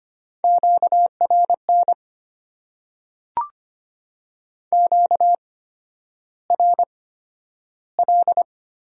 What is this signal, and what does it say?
Morse code, heard as a keyed tone, about 25 wpm, spaced as for 11 wpm, 690 Hz. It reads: QRN QRL
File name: Top 90 QSO Elements - 8x Character Spacing - 25wpm